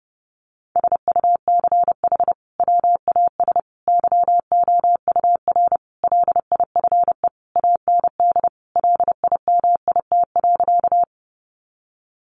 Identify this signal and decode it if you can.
Morse code, heard as a keyed tone, about 30 wpm, 700 Hz. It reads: SUCH WAS YOUR LIFE AND LIMIT.
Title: Sentences from Top 1500 Words 30wpm